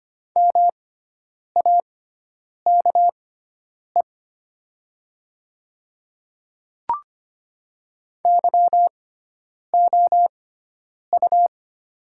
Morse code, heard as a keyed tone, about 25 wpm, spaced as for 9 wpm, 700 Hz. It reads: MAKE YOU
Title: 6x Character Spacing - Top 100 Words - 25wpm